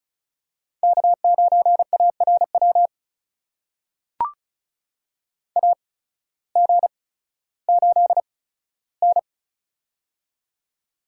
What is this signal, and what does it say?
Morse code, heard as a keyed tone, about 36 wpm, spaced as for 16 wpm, 700 Hz. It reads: K9ARW AG8N
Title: USA Call Signs - 8x Character Spacing - 35wpm